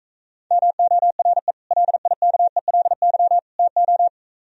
Morse code, heard as a keyed tone, about 44 wpm, 700 Hz. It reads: MORE LIKELY TO